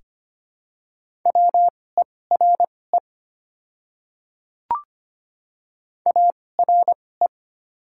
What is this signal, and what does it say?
Morse code, heard as a keyed tone, about 25 wpm, spaced as for 18 wpm, 700 Hz. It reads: WERE ARE
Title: 2x Character Spacing - Top 100 Words - 25wpm